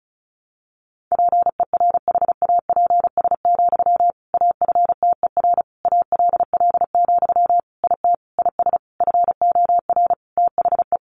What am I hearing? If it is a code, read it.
Morse code, heard as a keyed tone, about 34 wpm, 700 Hz. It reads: PERHAPS, AFTER ALL, IT IS FOR THE
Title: Sentences from Top 500 Words 35wpm